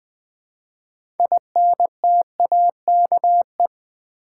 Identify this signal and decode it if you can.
Morse code, heard as a keyed tone, about 20 wpm, 700 Hz. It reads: INTAKE
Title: 6 Letter Words - First 3 Letters Given - 20wpm